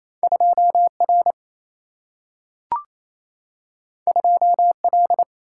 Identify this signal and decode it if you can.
Morse code, heard as a keyed tone, about 28 wpm, 700 Hz. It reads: 2 R 2 L
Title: Single-Letter Single-Number 28wpm